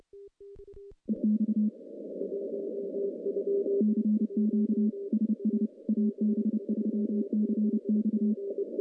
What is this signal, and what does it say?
Morse code, heard as a keyed tone, about 30 wpm, 210 Hz. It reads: EX CQ SS AB3CX
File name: contests / CW-SS-k5tr- -1300